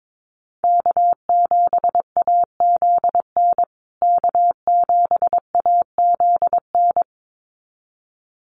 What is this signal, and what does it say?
Morse code, heard as a keyed tone, about 22 wpm, 700 Hz. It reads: K7AZN K7AZN
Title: USA Call Signs Repeated 2x 22wpm